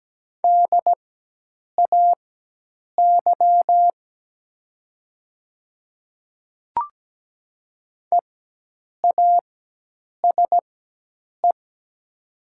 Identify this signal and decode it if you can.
Morse code, heard as a keyed tone, about 17 wpm, spaced as for 8 wpm, 700 Hz. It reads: DAY EASE
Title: 4x Character Spacing - Top 500 Words - 17wpm